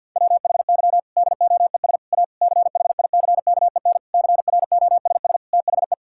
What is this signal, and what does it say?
Morse code, heard as a keyed tone, about 51 wpm, 700 Hz. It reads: WHY DOES A CHICKEN CROSS THE
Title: Sentences from Top 1000 Words - Review Part 5 - 50wpm